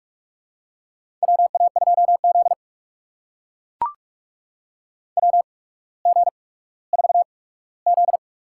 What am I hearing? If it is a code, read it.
Morse code, heard as a keyed tone, about 42 wpm, spaced as for 19 wpm, 700 Hz. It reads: WA2Z WG4Z
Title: USA Call Signs - 4 Characters - 8x Character Spacing - 45wpm